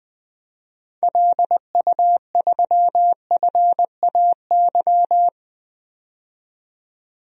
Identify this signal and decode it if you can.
Morse code, heard as a keyed tone, about 20 wpm, 700 Hz. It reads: LU3FAY
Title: DXCC-100 Call Signs - 6 Characters - 20wpm